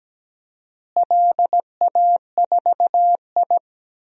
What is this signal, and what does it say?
Morse code, heard as a keyed tone, about 17 wpm, 700 Hz. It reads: LA4I
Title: DXCC-100 Call Signs - 4 Characters - Repeated 3x - 17wpm